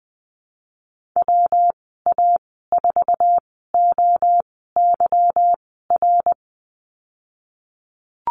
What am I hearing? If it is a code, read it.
Morse code, heard as a keyed tone, about 20 wpm, spaced as for 15 wpm, 700 Hz. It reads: WA4OYR
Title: 179 Lesson 29g - Slash - Call Signs 20wpm